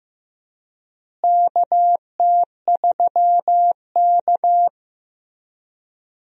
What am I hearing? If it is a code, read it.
Morse code, heard as a keyed tone, about 15 wpm, 700 Hz. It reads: KT3K
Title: USA Call Signs - 4 Characters - Repeated 2x - 15wpm